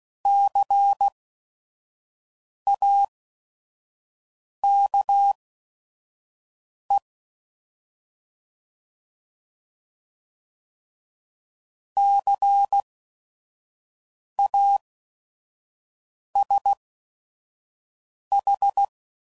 Morse code, heard as a keyed tone, about 16 wpm, spaced as for 5 wpm, 800 Hz. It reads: CAKE CASH
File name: Prefixes / By First Letter / Cs